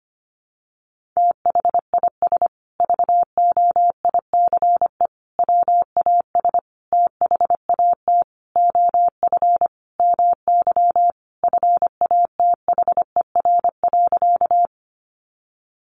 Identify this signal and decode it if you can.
Morse code, heard as a keyed tone, about 25 wpm, 700 Hz. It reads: THIS VOICE WAS THAT OF MY FATHER.
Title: Sentences from Top 500 Words 25wpm